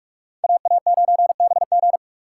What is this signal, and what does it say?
Morse code, heard as a keyed tone, about 44 wpm, 700 Hz. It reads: AA9BG